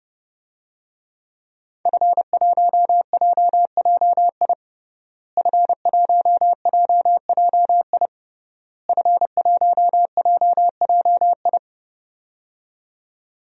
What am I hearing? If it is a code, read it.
Morse code, heard as a keyed tone, about 30 wpm, 700 Hz. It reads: F1JJI F1JJI F1JJI
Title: DXCC-100 Call Signs - 5 Characters - Repeated 3x - 30wpm